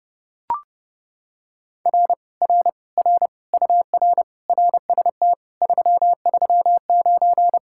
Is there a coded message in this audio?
Yes: Morse code from 1.9 to 7.6 s, heard as a keyed tone, about 28 wpm, 700 Hz. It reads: R R R UR RST 339